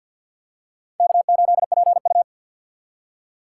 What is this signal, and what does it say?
Morse code, heard as a keyed tone, about 48 wpm, 700 Hz. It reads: K8PU
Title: USA Call Signs - 2x Character Spacing - 50wpm